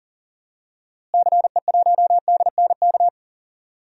Morse code, heard as a keyed tone, about 40 wpm, 700 Hz. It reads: CE1DNK
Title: DXCC-100 Call Signs - 6 Characters - Repeated 2x - 40wpm